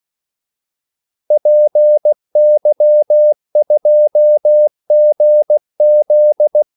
Morse code, heard as a keyed tone, about 16 wpm, 600 Hz. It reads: PY2GZ
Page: Kaldesignaler 16 wpm